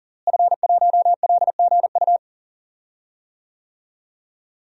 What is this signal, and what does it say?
Morse code, heard as a keyed tone, about 40 wpm, 700 Hz. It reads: F1LGU
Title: DXCC-100 Call Signs - 5 Characters - Repeated 2x - 40wpm